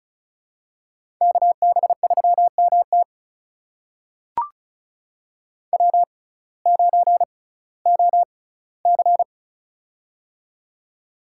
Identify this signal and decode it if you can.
Morse code, heard as a keyed tone, about 36 wpm, spaced as for 19 wpm, 700 Hz. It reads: KB3MT W9OC
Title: USA Call Signs - 6x Character Spacing - 35wpm